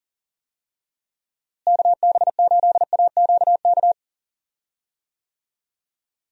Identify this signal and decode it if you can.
Morse code, heard as a keyed tone, about 40 wpm, 700 Hz. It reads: KB8AQK
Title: USA Call Signs - 6 Characters - Repeated 3x - 40wpm